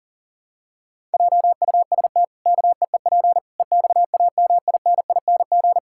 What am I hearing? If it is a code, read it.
Morse code, heard as a keyed tone, about 40 wpm, 700 Hz. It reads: JUST KEEP EXAMINING
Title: Pangram Challenge v2 - 40wpm